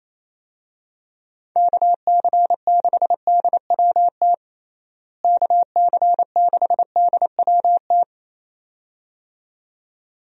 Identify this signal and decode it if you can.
Morse code, heard as a keyed tone, about 28 wpm, 700 Hz. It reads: KC6DWT KC6DWT
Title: USA Call Signs - 6 Characters - Repeated 2x - 28wpm